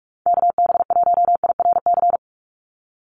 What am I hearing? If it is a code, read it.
Morse code, heard as a keyed tone, about 46 wpm, 700 Hz. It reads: KB1IRC
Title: USA Call Signs 45wpm